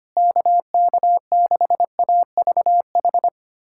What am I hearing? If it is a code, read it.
Morse code, heard as a keyed tone, about 25 wpm, 700 Hz. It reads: KK6AVH
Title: USA Callsign DE Callsign - 6 Characters - 25wpm